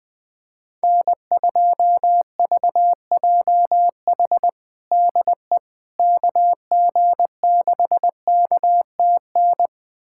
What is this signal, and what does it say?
Morse code, heard as a keyed tone, about 20 wpm, 700 Hz. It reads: N2VJH DE KG6KTN
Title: USA Callsign DE Callsign 20wpm